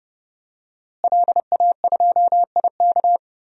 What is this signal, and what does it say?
Morse code, heard as a keyed tone, about 30 wpm, 700 Hz. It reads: LA2IK
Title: DXCC-100 Call Signs - 5 Characters - Repeated 2x - 30wpm